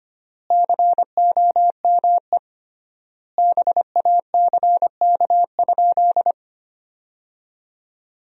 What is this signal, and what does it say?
Morse code, heard as a keyed tone, about 25 wpm, 700 Hz. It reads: COME BACK?
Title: 3x Word-Spacing - Sentences from Top 200 Words - 25wpm